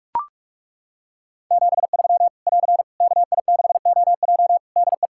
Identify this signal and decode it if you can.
Morse code, heard as a keyed tone, about 42 wpm, 700 Hz. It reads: 73 <AR> KI6OJ DE
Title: K1USN SST QSO Practice 45wpm